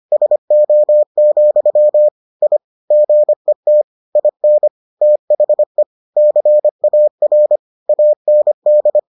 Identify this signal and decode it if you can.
Morse code, heard as a keyed tone, about 25 wpm, 600 Hz. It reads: SO, I GET IN THE CAR AND